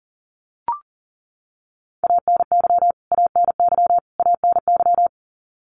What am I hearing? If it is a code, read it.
Morse code, heard as a keyed tone, about 40 wpm, 700 Hz. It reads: ANY ANY ANY
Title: Top 500 Words Repeated 40wpm